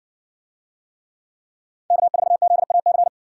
Morse code, heard as a keyed tone, about 67 wpm, 700 Hz. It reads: K4ZAC